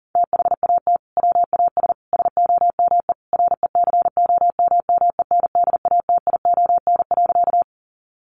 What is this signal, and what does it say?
Morse code, heard as a keyed tone, about 40 wpm, 700 Hz. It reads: THAT WAS SOME RECOMMENDATION.
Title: Sentences from Top 1400 Words 40wpm